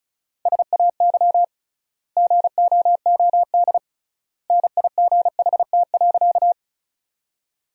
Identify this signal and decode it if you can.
Morse code, heard as a keyed tone, about 36 wpm, 700 Hz. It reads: SAY GOOD NIGHT.
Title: 3x Word-Spacing - Sentences from Top 300 Words - 35wpm